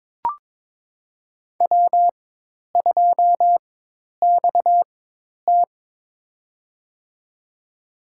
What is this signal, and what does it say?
Morse code, heard as a keyed tone, about 22 wpm, spaced as for 10 wpm, 700 Hz. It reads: W2XT
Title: USA Call Signs - 4 Characters - 4x Character Spacing - 22wpm